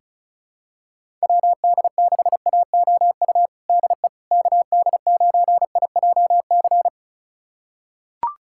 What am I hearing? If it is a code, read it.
Morse code, heard as a keyed tone, about 36 wpm, 700 Hz. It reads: WD6AOU DE KD9IJC
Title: USA Callsign DE Callsign - 6 Characters - 35wpm